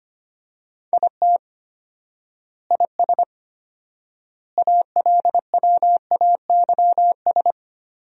Morse code, heard as a keyed tone, about 25 wpm, 700 Hz. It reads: IT IS ALWAYS
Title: 4x Word-Spacing - Sentences from Top 300 Words - 25wpm